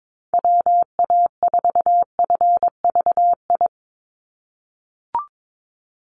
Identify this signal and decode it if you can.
Morse code, heard as a keyed tone, about 22 wpm, 700 Hz. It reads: WA4FVI
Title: USA Call Signs 22wpm